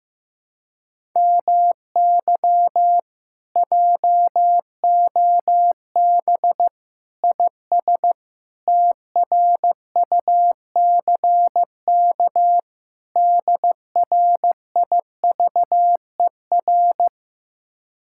Morse code, heard as a keyed tone, about 16 wpm, 700 Hz. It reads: MY JOB IS TRUCK DRIVER
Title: Speed-Racing Common QSO Phrases Encore - 15wpm